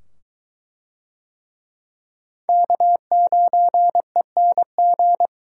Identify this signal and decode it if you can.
Morse code, heard as a keyed tone, about 21 wpm, 700 Hz. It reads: K9ENG